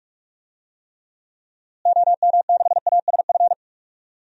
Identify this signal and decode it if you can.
Morse code, heard as a keyed tone, about 46 wpm, 700 Hz. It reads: OM6ASF
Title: DXCC-100 Call Signs - 6 Characters - Repeated 3x - 45wpm